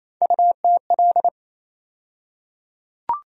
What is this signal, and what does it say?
Morse code, heard as a keyed tone, about 27 wpm, 700 Hz. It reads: UTL
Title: USA Call Sign Postfixes - 3 Letters 28wpm